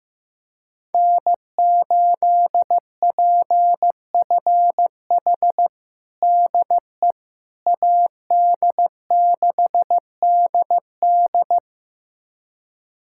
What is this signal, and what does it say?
Morse code, heard as a keyed tone, about 16 wpm, 700 Hz. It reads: N8PFH DE AD6DD
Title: USA Callsign DE Callsign - 5 Characters - 15wpm